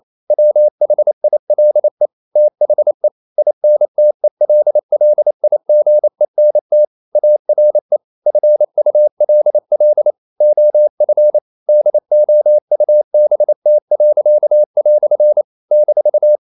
Morse code, heard as a keyed tone, about 28 wpm, 600 Hz. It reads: WHILE THE INTELLIGENT ARE FULL OF DOUBT." -
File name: CW / Practice / EnglishQuotations0128